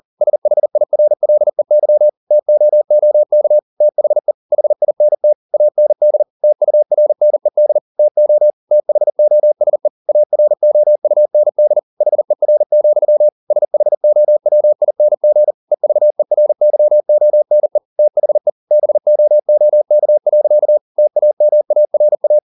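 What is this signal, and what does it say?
Morse code, heard as a keyed tone, about 40 wpm, 600 Hz. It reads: SHIRLEY TOOK THE HINT AND TURNED TO THOSE AROUND HER, SHOWING EVERYONE THE BOOK. TAMARA
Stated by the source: Morse code at 40 wpm, practice text, CW med Gnister.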